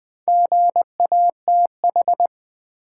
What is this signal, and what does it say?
Morse code, heard as a keyed tone, about 20 wpm, 690 Hz. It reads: GATH
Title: Build A Word - 2 Letters At A Time - 6-Letter Words - 20wpm